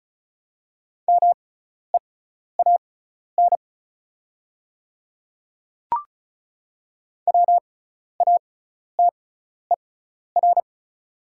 Morse code, heard as a keyed tone, about 36 wpm, spaced as for 12 wpm, 700 Hz. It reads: MEAN WATER
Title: 6x Character Spacing - Top 200 Words - 35wpm